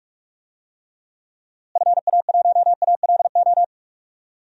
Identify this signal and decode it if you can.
Morse code, heard as a keyed tone, about 41 wpm, 700 Hz. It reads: FA1ALO
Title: DXCC-100 Call Signs - 6 Characters - Repeated 2x - 45wpm